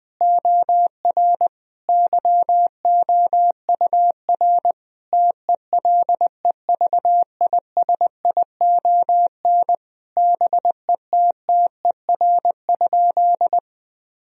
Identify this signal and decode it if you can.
Morse code, heard as a keyed tone, about 19 wpm, 700 Hz. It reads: OR YOUR TELEVISION BETTER?
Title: Sentences from Top 1000 Words - Review Part 4 - 20wpm